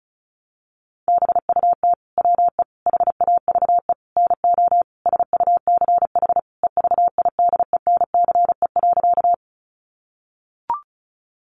Morse code, heard as a keyed tone, about 34 wpm, 700 Hz. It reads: BUT WE HAVE NO SUCH EVIDENCE.
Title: Sentences from Top 1300 Words 35wpm